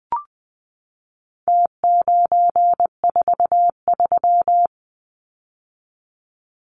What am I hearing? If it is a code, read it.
Morse code, heard as a keyed tone, about 20 wpm, 700 Hz. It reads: T943
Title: Numbers - Contest Style - 4-Digit 20wpm